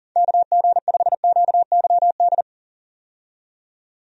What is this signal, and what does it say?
Morse code, heard as a keyed tone, about 40 wpm, 700 Hz. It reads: KG5QYD
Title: USA Call Signs - 6 Characters - 40wpm